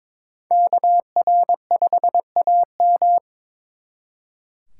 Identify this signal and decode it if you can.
Morse code, heard as a keyed tone, about 25 wpm, 700 Hz. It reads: KR5AM
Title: Speed Racing - USA 2x2 Call Signs 22wpm